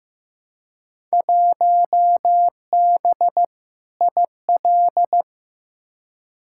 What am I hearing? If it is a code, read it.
Morse code, heard as a keyed tone, about 15 wpm, 700 Hz. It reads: 1B IL
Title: Field Day Exchanges - 15wpm